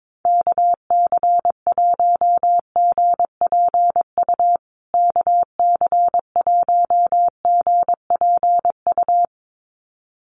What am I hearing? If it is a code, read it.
Morse code, heard as a keyed tone, about 22 wpm, 700 Hz. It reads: KC1GPU KC1GPU
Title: USA Call Signs Repeated 2x 22wpm